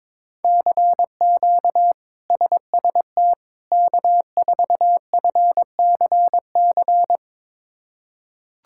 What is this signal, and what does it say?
Morse code, heard as a keyed tone, about 23 wpm, 700 Hz. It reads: CQ SST K4FCC
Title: K1USN SST QSO Practice - 22wpm